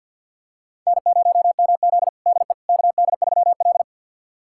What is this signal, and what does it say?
Morse code, heard as a keyed tone, about 50 wpm, 700 Hz. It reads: N0MZ DE KD3L
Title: USA Callsign DE Callsign - 4 Characters - 50wpm